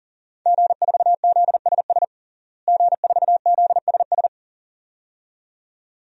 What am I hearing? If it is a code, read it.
Morse code, heard as a keyed tone, about 40 wpm, 700 Hz. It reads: G4ZSS G4ZSS
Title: DXCC-100 Call Signs - 5 Characters - Repeated 2x - 40wpm